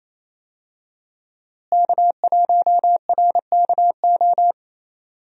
Morse code, heard as a keyed tone, about 28 wpm, 700 Hz. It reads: K1RKO
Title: USA Call Signs - 5 Characters - Repeated 2x - 28wpm